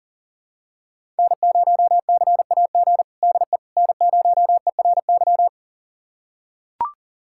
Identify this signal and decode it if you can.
Morse code, heard as a keyed tone, about 40 wpm, 700 Hz. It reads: N0CAG DE N0ERY
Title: USA Callsign DE Callsign - 5 Characters - 40wpm